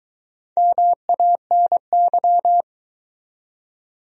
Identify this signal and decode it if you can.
Morse code, heard as a keyed tone, about 22 wpm, 700 Hz. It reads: MANY